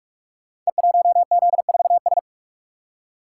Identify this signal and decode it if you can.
Morse code, heard as a keyed tone, about 44 wpm, 700 Hz. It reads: E1Z4S
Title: Code Groups - 5 Letter:Numbers - 45wpm